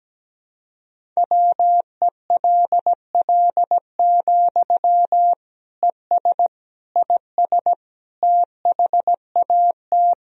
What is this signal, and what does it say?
Morse code, heard as a keyed tone, about 17 wpm, 700 Hz. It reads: WELL, ES IS THAT
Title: Sentences from Top 200 Words and Abbreviations 17wpm